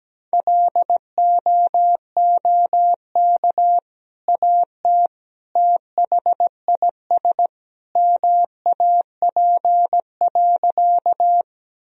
Morse code, heard as a keyed tone, about 17 wpm, 700 Hz. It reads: LOOK AT THIS MAP.
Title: Sentences from Top 1000 Words - Review Part 2 - 17wpm